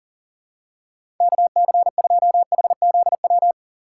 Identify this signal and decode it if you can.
Morse code, heard as a keyed tone, about 40 wpm, 700 Hz. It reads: KC2HZW
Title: USA Call Signs - 6 Characters - 3x Character Spacing - 40wpm